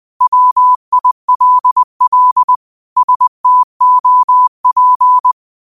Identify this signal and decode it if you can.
Morse code, heard as a keyed tone, about 20 wpm, 1000 Hz. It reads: WILL STOP